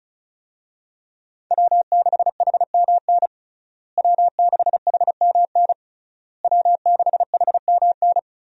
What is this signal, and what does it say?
Morse code, heard as a keyed tone, about 34 wpm, 700 Hz. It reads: W6HMN W6HMN W6HMN
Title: USA Call Signs - 5 Characters - Repeated 3x - 35wpm